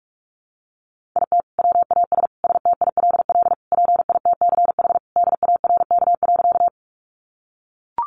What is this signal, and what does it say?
Morse code, heard as a keyed tone, about 44 wpm, 700 Hz. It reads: IT WAS STILL PITCH DARK.
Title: Sentences from Top 1300 Words 45wpm